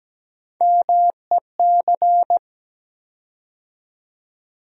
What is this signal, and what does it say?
Morse code, heard as a keyed tone, about 17 wpm, 700 Hz. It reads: MEC
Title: USA Call Sign Postfixes - 3 Letters 17wpm